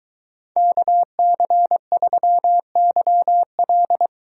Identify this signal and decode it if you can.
Morse code, heard as a keyed tone, about 22 wpm, 700 Hz. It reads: KC3YL